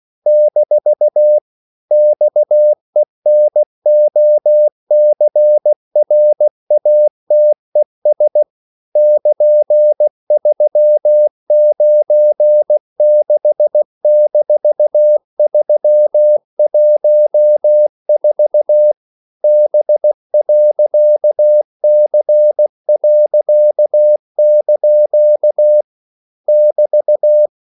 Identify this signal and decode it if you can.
Morse code, heard as a keyed tone, about 16 wpm, 600 Hz. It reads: - XENOCRATES <KN>396-314 B.C.) <BT>
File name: CW / Practice / EnglishQuotations0216